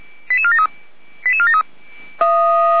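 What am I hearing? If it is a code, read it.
Five-tone selective call (ZVEI), heard as quick runs of five tones, tones of about 70 ms. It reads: 88473
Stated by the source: Five-tone ZVEI selective call received on an SDR.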